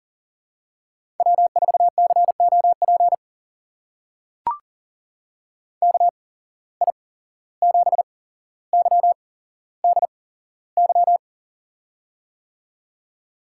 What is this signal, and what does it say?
Morse code, heard as a keyed tone, about 40 wpm, spaced as for 16 wpm, 700 Hz. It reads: W4COP KI7YDY
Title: USA Call Signs - 8x Character Spacing - 40wpm